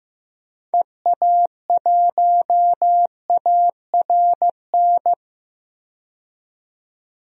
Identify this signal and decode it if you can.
Morse code, heard as a keyed tone, about 15 wpm, 700 Hz. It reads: EA1ARN
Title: DXCC-100 Call Signs - 6 Characters - 15wpm